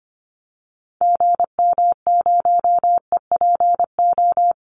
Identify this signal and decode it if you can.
Morse code, heard as a keyed tone, about 25 wpm, 700 Hz. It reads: GM0EPO
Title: Call Signs Repeated 2x - 25wpm